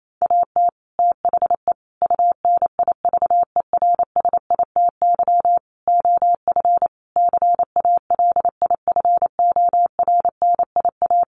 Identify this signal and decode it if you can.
Morse code, heard as a keyed tone, about 28 wpm, 700 Hz. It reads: AT THE UNIVERSITY OF CALIFORNIA
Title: Sets of 5 Words Encore2 28wpm